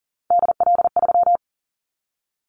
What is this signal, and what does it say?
Morse code, heard as a keyed tone, about 38 wpm, 700 Hz. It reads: DL3
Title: Callsign Prefixes for Top 25 DXCC - 40wpm